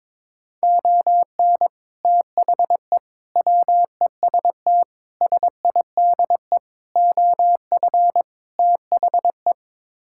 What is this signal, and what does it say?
Morse code, heard as a keyed tone, about 22 wpm, 700 Hz. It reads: ON THE WEST SIDE OF THE
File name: practice / 22wpm / Sets of 6 Words - 22wpm